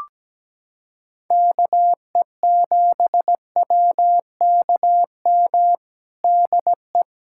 Morse code, heard as a keyed tone, about 17 wpm, 700 Hz. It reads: KE7WKM DE
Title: USA Callsign DE Callsign - 6 Characters - 17wpm